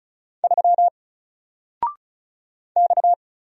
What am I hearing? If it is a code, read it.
Morse code, heard as a keyed tone, about 35 wpm, 700 Hz. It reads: 3 X
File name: Single-Letter-Number-35wpm